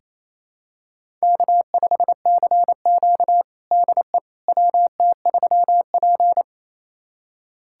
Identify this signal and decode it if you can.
Morse code, heard as a keyed tone, about 28 wpm, 700 Hz. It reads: K5CQ DE WT3P
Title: USA Callsign DE Callsign - 4 Characters - 28wpm